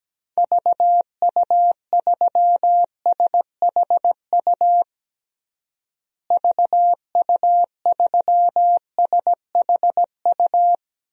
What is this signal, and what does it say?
Morse code, heard as a keyed tone, about 17 wpm, 700 Hz. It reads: VU3SHU VU3SHU
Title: DXCC-100 Call Signs - 6 Characters - Repeated 2x - 17wpm